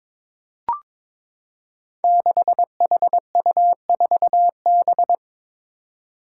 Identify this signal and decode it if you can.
Morse code, heard as a keyed tone, about 22 wpm, 700 Hz. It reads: 6HU4B
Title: Five Challenging Letter-Number Combinations 22wpm